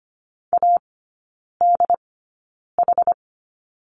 Morse code, heard as a keyed tone, about 25 wpm, spaced as for 9 wpm, 690 Hz. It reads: ADH
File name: Lesson 14e - Number 2 - Three Characters 25wpm